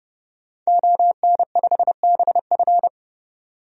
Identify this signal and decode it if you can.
Morse code, heard as a keyed tone, about 30 wpm, 700 Hz. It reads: ON5BF
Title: DXCC-100 Call Signs - 5 Characters - 30wpm